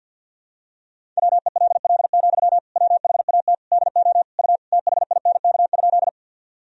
Morse code, heard as a keyed tone, about 50 wpm, 700 Hz. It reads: WELL, WHAT DO U THINK?
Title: Sentences from Top 200 Words and Abbreviations 50wpm